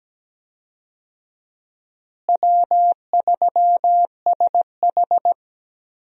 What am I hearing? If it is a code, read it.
Morse code, heard as a keyed tone, about 17 wpm, 700 Hz. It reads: W3SH
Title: USA Call Signs - 4 Characters - Repeated 3x - 17wpm